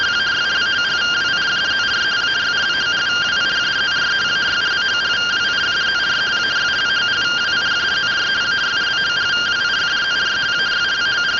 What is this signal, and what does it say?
Two-tone teleprinter signal, about 170 Hz shift.